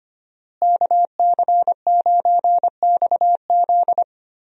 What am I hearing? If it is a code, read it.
Morse code, heard as a keyed tone, about 25 wpm, 700 Hz. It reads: KC9XZ
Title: USA Call Signs - 5 Characters - Repeated 2x - 25wpm